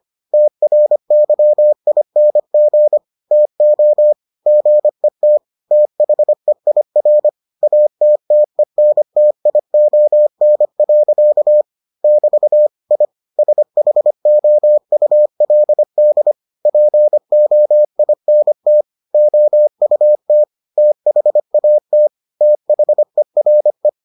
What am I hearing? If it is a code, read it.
Morse code, heard as a keyed tone, about 25 wpm, 600 Hz. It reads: TRYING TO GET THEIR ATTENTION. <BT> I SHOULD POINT OUT THAT THERE